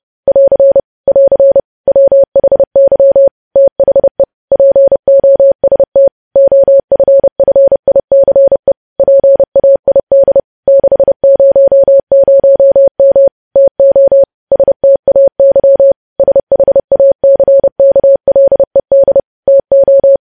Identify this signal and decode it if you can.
Morse code, heard as a keyed tone, about 30 wpm, 550 Hz. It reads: <AR> <AR> WHY THE POST OFFICE PAID 600M TO STAY SHACKLED TO
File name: podcast / News headlines-1x-30x30